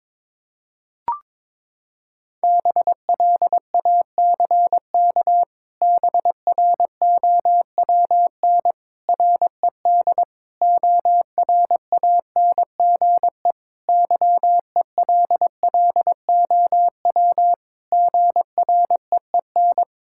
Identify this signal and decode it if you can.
Morse code, heard as a keyed tone, about 22 wpm, 700 Hz. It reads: BLACK BROWN RED ORANGE YELLOW GREEN
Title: Pangram Challenge v2 - 22wpm